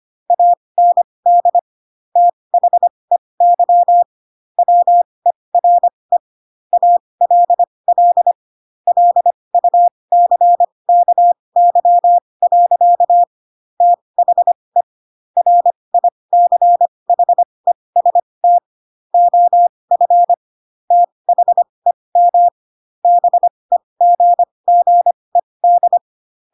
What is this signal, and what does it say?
Morse code, heard as a keyed tone, about 25 wpm, spaced as for 20 wpm, 700 Hz. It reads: AND THEY WERE ALL LUCKY. THE RICHEST OF THEM BEGGED